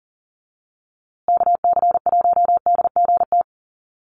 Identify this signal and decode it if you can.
Morse code, heard as a keyed tone, about 40 wpm, 700 Hz. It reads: KC1DGT
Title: USA Call Signs Repeated 3x 40wpm